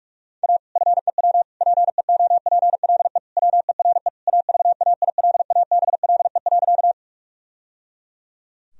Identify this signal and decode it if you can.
Morse code, heard as a keyed tone, about 43 wpm, 700 Hz. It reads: A FEW PEOPLE WERE AVAILABLE.
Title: Sentences from Top 1000 Words - Review Part 3 - 45wpm